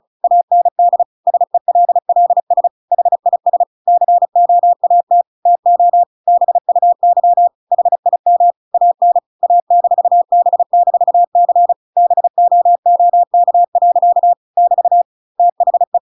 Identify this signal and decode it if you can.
Morse code, heard as a keyed tone, about 34 wpm, 700 Hz. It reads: AND SELLS HIS COAT TO BUY HIM AN A-B-C BOOK. <BT> THE
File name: Pinocchio / Pinocchio ch8 35 35 ik1hge